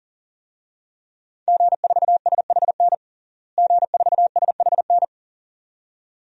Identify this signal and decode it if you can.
Morse code, heard as a keyed tone, about 37 wpm, 700 Hz. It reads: G4SHN G4SHN
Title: DXCC-100 Call Signs - 5 Characters - Repeated 2x - 40wpm